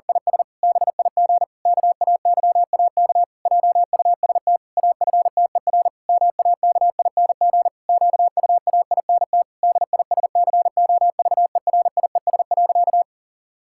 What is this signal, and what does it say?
Morse code, heard as a keyed tone, about 40 wpm, 700 Hz. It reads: IS BIG KAYAK JUST AFTER MAKING QUAINT DISCOVERIES.
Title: Pangram Challenge v2 - 40wpm